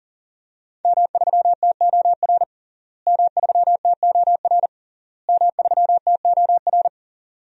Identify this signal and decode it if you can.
Morse code, heard as a keyed tone, about 40 wpm, 700 Hz. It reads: M3TOR M3TOR M3TOR
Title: DXCC-100 Call Signs - 5 Characters - Repeated 3x - 40wpm